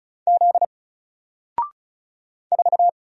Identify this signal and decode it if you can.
Morse code, heard as a keyed tone, about 36 wpm, 700 Hz. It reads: Z 4
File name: Single-Letter-Number-35wpm